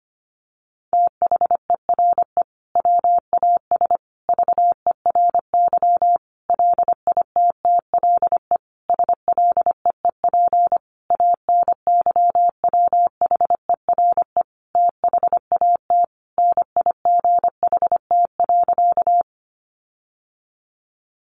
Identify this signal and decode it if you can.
Morse code, heard as a keyed tone, about 25 wpm, 700 Hz. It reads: THERE WAS VERY LITTLE SLEEP ANYWHERE THAT NIGHT.
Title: Sentences from Top 1400 Words 25wpm